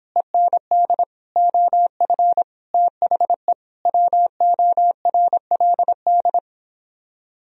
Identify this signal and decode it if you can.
Morse code, heard as a keyed tone, about 26 wpm, 700 Hz. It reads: END OF THE WORLD